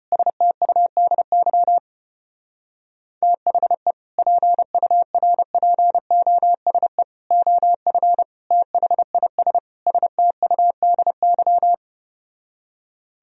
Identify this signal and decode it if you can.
Morse code, heard as a keyed tone, about 33 wpm, 700 Hz. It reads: STUDY THE PURPOSE OF THIS STUDY